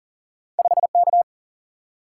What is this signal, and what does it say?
Morse code, heard as a keyed tone, about 40 wpm, 700 Hz. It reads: 5 K
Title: Single-Letter Single-Number 40wpm